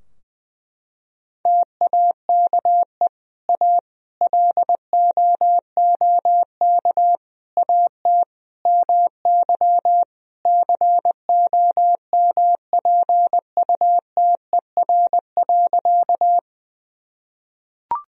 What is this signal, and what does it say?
Morse code, heard as a keyed tone, about 21 wpm, 700 Hz. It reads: TAKE A LOOK AT MY COMPUTER.
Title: Sentences from Top 1000 Words - Review Part 3 - 20wpm